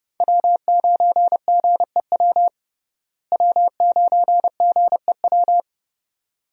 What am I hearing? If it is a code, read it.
Morse code, heard as a keyed tone, about 30 wpm, 700 Hz. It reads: W9GEW W9GEW
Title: USA Call Signs - 5 Characters - Repeated 2x - 30wpm